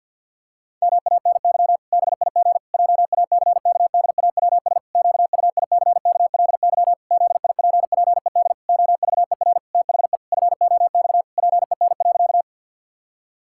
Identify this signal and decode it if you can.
Morse code, heard as a keyed tone, about 50 wpm, 700 Hz. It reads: MANY BIG JACKDAWS QUICKLY ZIPPED OVER THE FOX PEN.
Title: Pangram Challenge v2 - 50wpm